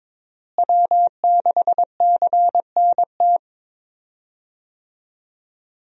Morse code, heard as a keyed tone, about 22 wpm, 700 Hz. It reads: W6CNT
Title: USA Call Signs - 5 Characters - Repeated 2x - 22wpm